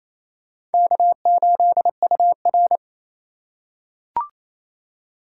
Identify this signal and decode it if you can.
Morse code, heard as a keyed tone, about 28 wpm, 700 Hz. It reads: K8UR
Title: USA Call Signs - 4x Character Spacing - 28wpm